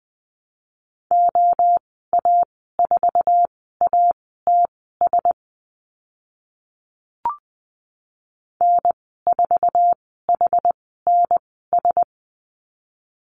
Morse code, heard as a keyed tone, about 20 wpm, spaced as for 14 wpm, 700 Hz. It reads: OA4ATS N4HNS
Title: 062 Lesson 11g - Letter h - Call Signs 20wpm